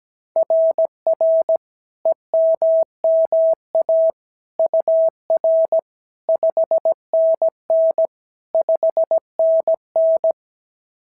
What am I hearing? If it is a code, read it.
Morse code, heard as a keyed tone, about 16 wpm, 650 Hz. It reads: RR EMMA UR 5NN 5NN
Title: SOTA QSO Practice 17wpm